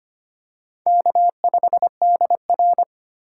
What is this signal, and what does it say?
Morse code, heard as a keyed tone, about 25 wpm, 700 Hz. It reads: K5DR
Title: USA Call Signs - 4 Characters - 3x Character Spacing - 25wpm